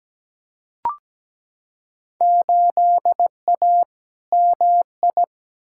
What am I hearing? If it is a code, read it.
Morse code, heard as a keyed tone, about 17 wpm, 700 Hz. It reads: 8A MI
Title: Field Day Exchanges Encore - 17wpm